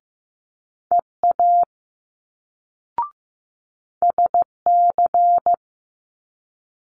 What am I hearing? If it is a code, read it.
Morse code, heard as a keyed tone, about 15 wpm, 700 Hz. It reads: EA SC
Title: Multi-letter Phonemes Quick Fire 15wpm